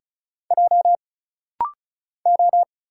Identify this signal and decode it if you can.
Morse code, heard as a keyed tone, about 34 wpm, 700 Hz. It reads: J O
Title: Single Letter-Number Alt - Rapid-Fire 35wpm